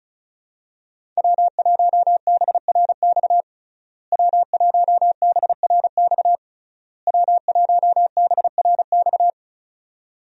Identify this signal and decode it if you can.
Morse code, heard as a keyed tone, about 34 wpm, 700 Hz. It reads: W1BRX W1BRX W1BRX
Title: USA Call Signs - 5 Characters - Repeated 3x - 35wpm